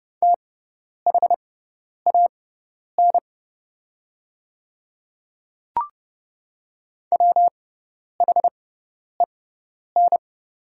Morse code, heard as a keyed tone, about 30 wpm, spaced as for 10 wpm, 700 Hz. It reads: THAN WHEN
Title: 6x Character Spacing - Top 100 Words - 30wpm